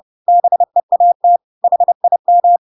Morse code, heard as a keyed tone, about 30 wpm, 700 Hz. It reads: BEAT HIM
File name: Pinocchio / Pinocchio ch3 30 30 ik1hge